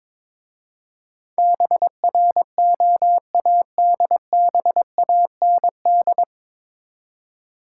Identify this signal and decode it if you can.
Morse code, heard as a keyed tone, about 23 wpm, 700 Hz. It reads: BROADBAND
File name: 9 Letter Words - First 3 Letters Given - 22wpm